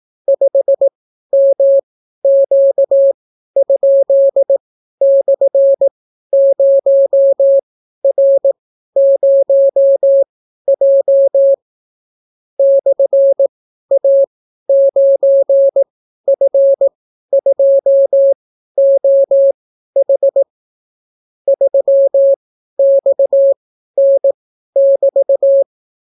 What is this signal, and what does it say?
Morse code, heard as a keyed tone, about 18 wpm, spaced as for 12 wpm, 550 Hz. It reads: 5MQ?/0R0J /A9F2OH 3XN<BT>